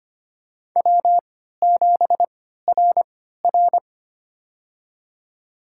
Morse code, heard as a keyed tone, about 25 wpm, spaced as for 14 wpm, 700 Hz. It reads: W7RR
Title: USA Call Signs - 4 Characters - 3x Character Spacing - 25wpm